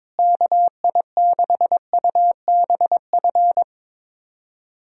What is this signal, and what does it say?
Morse code, heard as a keyed tone, about 22 wpm, 700 Hz. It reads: KI6UBF
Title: USA Call Signs - 6 Characters - Repeated 2x - 22wpm